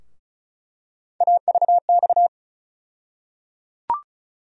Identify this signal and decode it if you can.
Morse code, heard as a keyed tone, about 36 wpm, 700 Hz. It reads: AVX
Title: USA Call Sign Postfixes - 3 Letters 35wpm